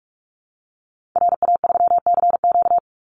Morse code, heard as a keyed tone, about 46 wpm, 700 Hz. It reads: RA3CQ
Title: Call Signs - 45wpm